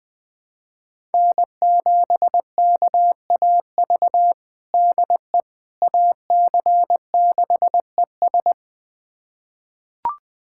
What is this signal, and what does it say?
Morse code, heard as a keyed tone, about 21 wpm, 700 Hz. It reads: N7KAV DE AC6ES
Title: USA Callsign DE Callsign - 5 Characters - 20wpm